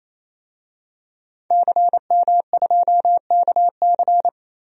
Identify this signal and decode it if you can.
Morse code, heard as a keyed tone, about 28 wpm, 700 Hz. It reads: CM2KC
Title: DXCC-100 Call Signs - 5 Characters - 28wpm